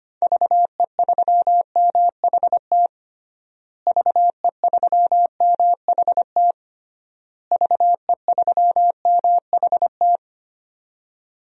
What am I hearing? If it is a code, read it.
Morse code, heard as a keyed tone, about 25 wpm, 700 Hz. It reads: VE3MHT VE3MHT VE3MHT
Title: DXCC-100 Call Signs - 6 Characters - Repeated 3x - 25wpm